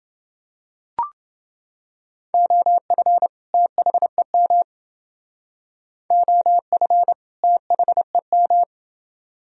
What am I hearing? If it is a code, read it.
Morse code, heard as a keyed tone, about 29 wpm, 700 Hz. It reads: OF THEM OF THEM